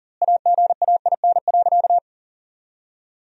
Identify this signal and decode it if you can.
Morse code, heard as a keyed tone, about 37 wpm, 700 Hz. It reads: AGAIN.
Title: 4x Word-Spacing - Sentences from Top 200 Words - 40wpm